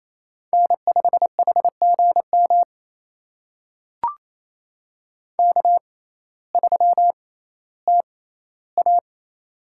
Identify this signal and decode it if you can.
Morse code, heard as a keyed tone, about 28 wpm, spaced as for 15 wpm, 700 Hz. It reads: N5HGM K3TA
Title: USA Call Signs - 6x Character Spacing - 28wpm